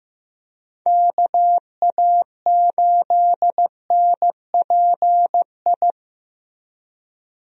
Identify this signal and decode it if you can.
Morse code, heard as a keyed tone, about 15 wpm, 700 Hz. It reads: KA8NPI
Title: USA Call Signs - 6 Characters - Repeated 2x - 15wpm